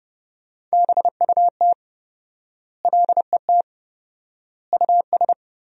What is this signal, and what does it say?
Morse code, heard as a keyed tone, about 30 wpm, 700 Hz. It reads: BUT LET US
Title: 4x Word-Spacing - Sentences from Top 500 Words - 30wpm